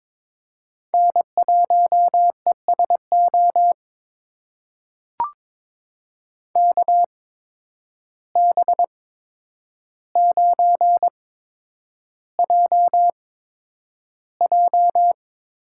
Morse code, heard as a keyed tone, about 22 wpm, spaced as for 9 wpm, 700 Hz. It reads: N1ESO KB9JJ
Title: USA Call Signs - 8x Character Spacing - 22wpm